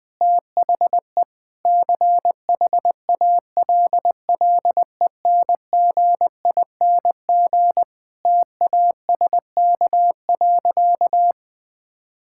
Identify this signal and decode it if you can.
Morse code, heard as a keyed tone, about 21 wpm, 700 Hz. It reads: THE CHALLENGING TASK.
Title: Difficult Sentences with Hints - 20wpm